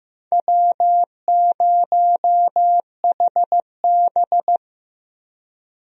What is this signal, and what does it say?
Morse code, heard as a keyed tone, about 15 wpm, 700 Hz. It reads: W0HB
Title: USA Call Signs - 4 Characters - Repeated 3x - 15wpm